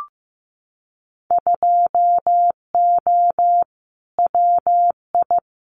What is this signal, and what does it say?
Morse code, heard as a keyed tone, about 15 wpm, 700 Hz. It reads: 2O WI
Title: Winter Field Day Exchanges 15wpm